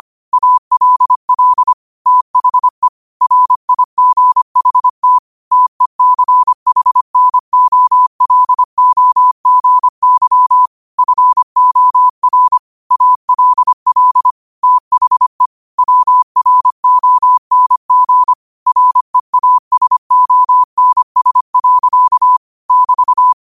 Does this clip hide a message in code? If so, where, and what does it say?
Yes: Morse code from 0.3 to 23.3 s, heard as a keyed tone, about 25 wpm, 1000 Hz. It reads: ALL THE RIGHT TECHNOLOGY FOR ALL THE WRONG REASONS. <BT>